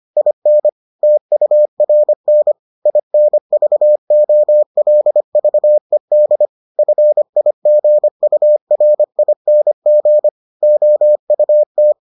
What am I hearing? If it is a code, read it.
Morse code, heard as a keyed tone, about 25 wpm, 600 Hz. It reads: IN TURN INVOLVED FIGURING OUT